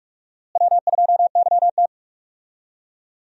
Morse code, heard as a keyed tone, about 45 wpm, 700 Hz. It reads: W2YT
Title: USA Call Signs - 4 Characters - Repeated 3x - 45wpm